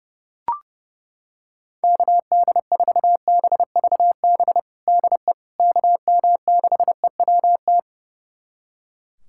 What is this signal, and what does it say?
Morse code, heard as a keyed tone, about 30 wpm, 700 Hz. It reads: KD4BVB DE KM6EWT
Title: USA Callsign DE Callsign - 6 Characters - 30wpm